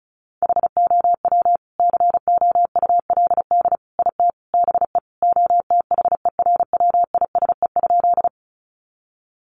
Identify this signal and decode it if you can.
Morse code, heard as a keyed tone, about 34 wpm, 700 Hz. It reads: HOW COULD IT BE OTHERWISE?
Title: Sentences from Top 1200 Words 35wpm